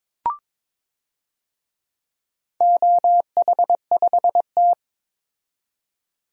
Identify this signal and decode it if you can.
Morse code, heard as a keyed tone, about 22 wpm, 700 Hz. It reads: OH5T
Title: DXCC-100 Call Signs - 4 Characters - 22wpm